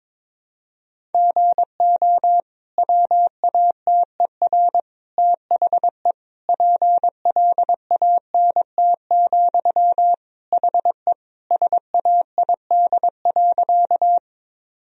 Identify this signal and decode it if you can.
Morse code, heard as a keyed tone, about 22 wpm, 700 Hz. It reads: GO WATER THE PLANT, HE SAID.
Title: Sentences from Top 1000 Words - Review Part 1 - 22wpm